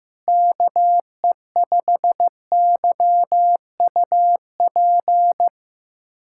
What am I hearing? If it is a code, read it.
Morse code, heard as a keyed tone, about 15 wpm, 700 Hz. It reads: KE5YUP
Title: USA Callsign DE Callsign - 6 Characters - 15wpm